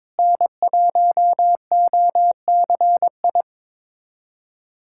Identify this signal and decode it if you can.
Morse code, heard as a keyed tone, about 22 wpm, 700 Hz. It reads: N1OCI
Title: USA Callsign DE Callsign 22wpm